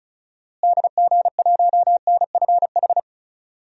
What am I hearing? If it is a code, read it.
Morse code, heard as a keyed tone, about 36 wpm, 700 Hz. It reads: DG1NFH
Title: DXCC-100 Call Signs - 6 Characters - Repeated 3x - 35wpm